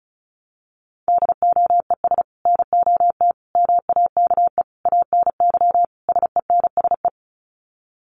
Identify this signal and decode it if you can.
Morse code, heard as a keyed tone, about 34 wpm, 700 Hz. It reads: DOES NOT MAKE ANY SENSE
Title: Sets of 5 Words Encore2 35wpm